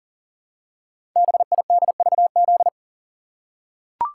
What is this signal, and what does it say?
Morse code, heard as a keyed tone, about 38 wpm, 690 Hz. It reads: BIDVZ
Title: Code Groups - 5 Letter:Numbers - 40wpm